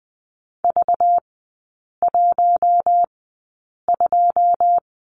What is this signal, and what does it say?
Morse code, heard as a keyed tone, about 20 wpm, spaced as for 8 wpm, 700 Hz. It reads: V12
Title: 212 Lesson 34e - Number 8 - Three Characters 20wpm